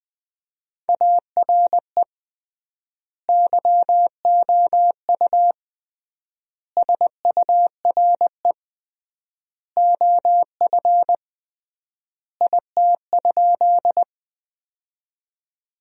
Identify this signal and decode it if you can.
Morse code, heard as a keyed tone, about 20 wpm, 700 Hz. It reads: ARE YOU SURE OF IT?
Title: 3x Word-Spacing - Sentences from Top 300 Words - 20wpm